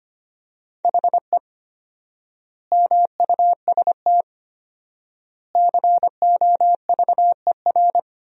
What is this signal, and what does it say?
Morse code, heard as a keyed tone, about 25 wpm, 700 Hz. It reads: HE MUST COVER
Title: 4x Word-Spacing - Sentences from Top 500 Words - 25wpm